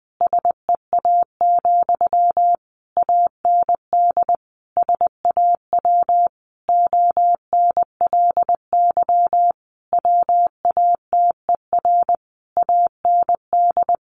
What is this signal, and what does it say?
Morse code, heard as a keyed tone, about 20 wpm, 700 Hz. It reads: SEA, AND SAW ONLY WATER AND
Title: Sentences from Top 1200 Words 20wpm